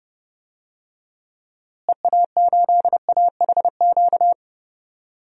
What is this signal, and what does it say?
Morse code, heard as a keyed tone, about 30 wpm, 700 Hz. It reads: EA8AHQ
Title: DXCC-100 Call Signs - 6 Characters - Repeated 3x - 30wpm